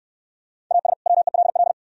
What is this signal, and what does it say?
Morse code, heard as a keyed tone, about 68 wpm, 700 Hz. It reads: AS WELL